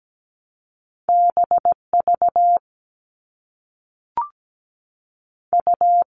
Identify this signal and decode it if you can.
Morse code, heard as a keyed tone, about 17 wpm, 700 Hz. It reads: BV U
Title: All Cut Numbers - 2-Digit 17wpm